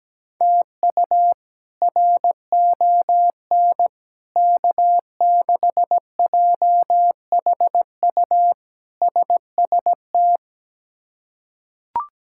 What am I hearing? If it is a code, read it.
Morse code, heard as a keyed tone, about 17 wpm, 700 Hz. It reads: TU RON K6JHU SST
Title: K1USN SST QSO Practice - 17wpm